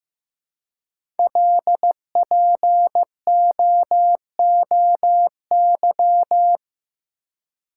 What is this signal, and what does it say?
Morse code, heard as a keyed tone, about 15 wpm, 700 Hz. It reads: LPOOY
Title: Code Groups - 5 Letters - 15wpm